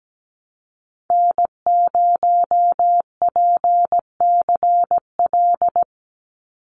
Morse code, heard as a keyed tone, about 17 wpm, 700 Hz. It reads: N0PCL
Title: NA SOTA Activator Callsigns 17wpm